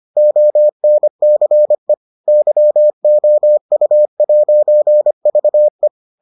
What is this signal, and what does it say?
Morse code, heard as a keyed tone, about 25 wpm, 600 Hz. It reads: ONCE YOU'VE